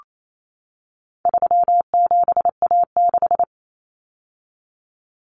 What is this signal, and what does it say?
Morse code, heard as a keyed tone, about 28 wpm, 700 Hz. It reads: 37A6
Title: Common Cut Numbers - 4-Digit 28wpm